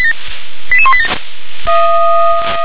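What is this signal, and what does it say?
Five-tone selective call (ZVEI), heard as quick runs of five tones, tones of about 70 ms. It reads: 88187
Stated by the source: ZVEI five-tone selective call 88126, received on SDR.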